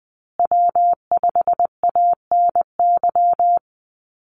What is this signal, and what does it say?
Morse code, heard as a keyed tone, about 20 wpm, 700 Hz. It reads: W5ANY
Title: USA Call Signs Repeated 3x 20wpm